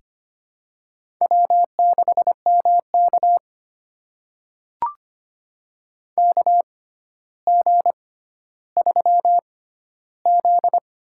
Morse code, heard as a keyed tone, about 25 wpm, spaced as for 13 wpm, 700 Hz. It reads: W6MK KG3Z
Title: USA Call Signs - 4 Characters - 6x Character Spacing - 25wpm